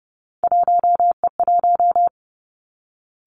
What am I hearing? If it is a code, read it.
Morse code, heard as a keyed tone, about 32 wpm, 700 Hz. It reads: 1E1
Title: Three Challenging Letter-Number Combinations 30wpm